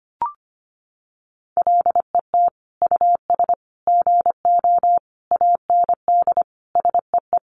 Morse code, heard as a keyed tone, about 25 wpm, 700 Hz. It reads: LET US GO AND SEE
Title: Sentences from Top 1100 Words 25wpm